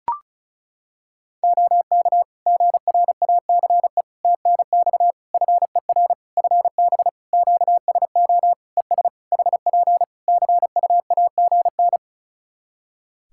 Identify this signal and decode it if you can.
Morse code, heard as a keyed tone, about 36 wpm, 700 Hz. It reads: OK GRACE TNX FER FB QSO ES HP CUAGN
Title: Practice QSO - Exchange 4 - Ending - 35wpm